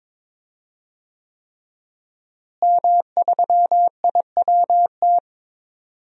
Morse code, heard as a keyed tone, about 22 wpm, 700 Hz. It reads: M3IWT
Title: DXCC-100 Call Signs - 5 Characters - Repeated 3x - 22wpm